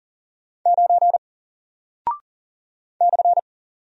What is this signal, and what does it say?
Morse code, heard as a keyed tone, about 40 wpm, 700 Hz. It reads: 9 /
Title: Single Character (Letter:Number:Punctuation:Prosign) Alt 40wpm